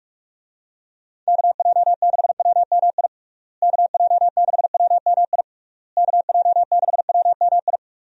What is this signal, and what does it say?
Morse code, heard as a keyed tone, about 41 wpm, 700 Hz. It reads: KJ6WMI KJ6WMI KJ6WMI
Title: USA Call Signs - 6 Characters - Repeated 3x - 45wpm